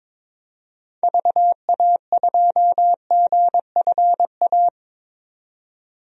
Morse code, heard as a keyed tone, about 22 wpm, 700 Hz. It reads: VA2GFA
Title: DXCC-100 Call Signs - 6 Characters - 22wpm